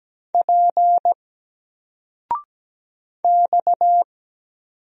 Morse code, heard as a keyed tone, about 17 wpm, 700 Hz. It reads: P X
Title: Single Character (Letter:Number:Punctuation:Prosign) 17wpm